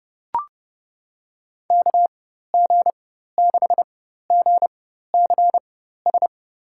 Morse code, heard as a keyed tone, about 32 wpm, spaced as for 14 wpm, 700 Hz. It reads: KG6GCS
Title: USA Call Signs - 6 Characters - 4x Character Spacing - 30wpm